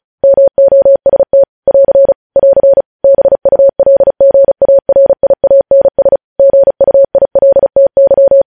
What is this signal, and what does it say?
Morse code, heard as a keyed tone, about 36 wpm, 550 Hz. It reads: MOST <AR> <AR> BULGARIANS GUILTY
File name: podcast / News headlines-1x-35x35